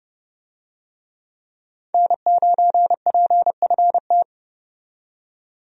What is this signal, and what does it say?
Morse code, heard as a keyed tone, about 32 wpm, 700 Hz. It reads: N9PFT
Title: USA Call Signs - 5 Characters - 30wpm